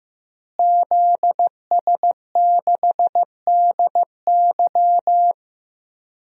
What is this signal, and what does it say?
Morse code, heard as a keyed tone, about 15 wpm, 700 Hz. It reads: ZS6DY
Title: DXCC-100 Call Signs - 5 Characters - Repeated 2x - 15wpm